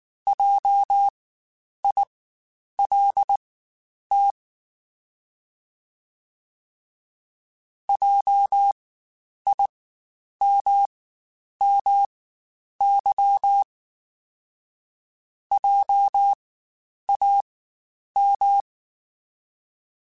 Morse code, heard as a keyed tone, about 19 wpm, spaced as for 9 wpm, 800 Hz. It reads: JILT JIMMY JAM
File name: Spaced Alphas / Spaced Alphas mp3s / Js sp @18